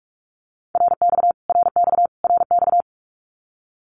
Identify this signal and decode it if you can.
Morse code, heard as a keyed tone, about 44 wpm, 700 Hz. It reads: RX RX RX
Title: Abbreviations Repeated 3x - 45wpm